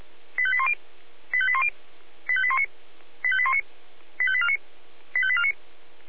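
Five-tone selective call (ZVEI), heard as quick runs of five tones, tones of about 70 ms. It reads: 85810 85810 85819 85819 85849 85849
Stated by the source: Five-tone ZVEI selective call received on an SDR.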